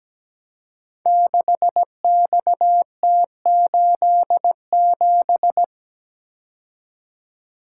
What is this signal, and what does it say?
Morse code, heard as a keyed tone, about 17 wpm, 700 Hz. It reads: 6XT87
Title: Code Groups - 5 Letter:Numbers - 17wpm